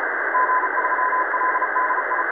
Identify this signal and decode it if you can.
Morse code, heard as a keyed tone, about 36 wpm, 1010 Hz, with radio noise. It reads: G3LDI
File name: 4U0IARU 14020.0kHz